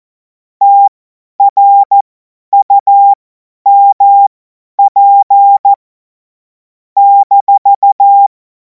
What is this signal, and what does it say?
Morse code, heard as a keyed tone, about 14 wpm, spaced as for 10 wpm, 800 Hz. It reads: TRUMP -
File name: Morsekurs / Witze / Witz 12. Mai 2026 50 F70